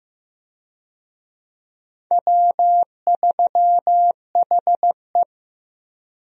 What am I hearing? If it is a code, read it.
Morse code, heard as a keyed tone, about 15 wpm, 700 Hz. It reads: W3HE
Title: USA Call Signs - 4 Characters - Repeated 3x - 15wpm